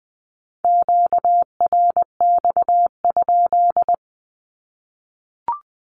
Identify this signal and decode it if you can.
Morse code, heard as a keyed tone, about 20 wpm, 700 Hz. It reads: QRX?
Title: Q-Codes 20wpm